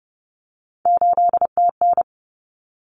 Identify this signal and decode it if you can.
Morse code, heard as a keyed tone, about 30 wpm, 700 Hz. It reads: 8TN
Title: Common Cut Numbers - 3-Digit 30wpm